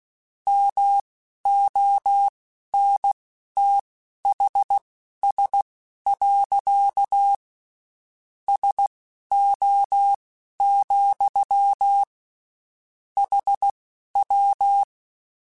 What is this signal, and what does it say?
Morse code, heard as a keyed tone, about 16 wpm, spaced as for 11 wpm, 800 Hz. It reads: MONTHS. SO, HW